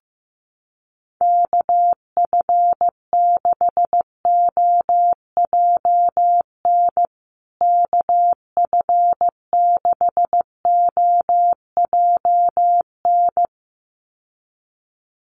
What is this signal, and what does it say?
Morse code, heard as a keyed tone, about 15 wpm, 700 Hz. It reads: KF6OJN KF6OJN
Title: USA Call Signs Repeated 2x 15wpm